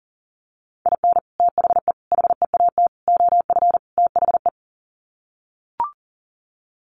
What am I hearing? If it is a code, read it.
Morse code, heard as a keyed tone, about 40 wpm, 700 Hz. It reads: IN THE HEAT OF THE
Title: Sets of 5 Words Encore2 40wpm